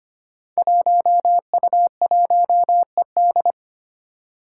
Morse code, heard as a keyed tone, about 25 wpm, 700 Hz. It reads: 1U1ED
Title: Five Challenging Letter-Number Combinations 25wpm